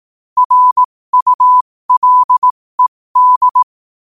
Morse code, heard as a keyed tone, about 18 wpm, spaced as for 15 wpm, 1000 Hz. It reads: RULED